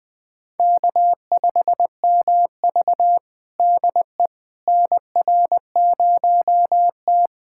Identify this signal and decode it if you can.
Morse code, heard as a keyed tone, about 20 wpm, 700 Hz. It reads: K5MV DE NR0T
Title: USA Callsign DE Callsign - 4 Characters - 20wpm